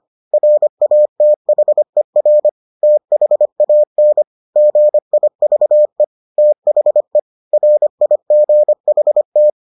Morse code, heard as a keyed tone, about 25 wpm, 600 Hz. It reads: RATHER THAN GIVE THE RIGHT